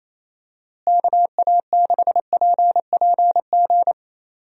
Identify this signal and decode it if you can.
Morse code, heard as a keyed tone, about 28 wpm, 700 Hz. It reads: KA6PPG
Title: USA Call Signs - 6 Characters - Repeated 2x - 28wpm